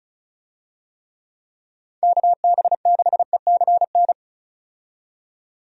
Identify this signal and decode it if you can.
Morse code, heard as a keyed tone, about 35 wpm, 700 Hz. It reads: KB6ECN
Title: USA Call Signs - 6 Characters - 35wpm